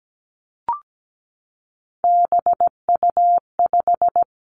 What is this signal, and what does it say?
Morse code, heard as a keyed tone, about 17 wpm, 700 Hz. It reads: BU5
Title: Three Challenging Letter-Number Combinations 17wpm